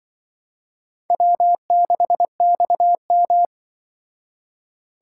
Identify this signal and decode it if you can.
Morse code, heard as a keyed tone, about 24 wpm, 700 Hz. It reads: W6XM
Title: Speed Racing - USA 1x2 Call Signs 20wpm